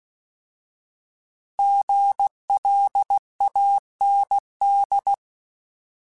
Morse code, heard as a keyed tone, about 16 wpm, 800 Hz. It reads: GLAND
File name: Alphabetical / Alphabetical mp3s / Gs @15